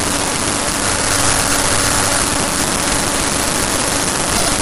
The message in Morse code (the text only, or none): none